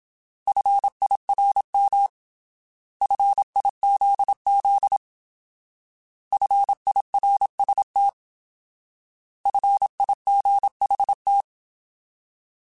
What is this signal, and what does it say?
Morse code, heard as a keyed tone, about 27 wpm, 800 Hz. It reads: FIRM FIZZ FIRST FIGHT
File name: Alphabetical / Alphabetical mp3s / Fs @25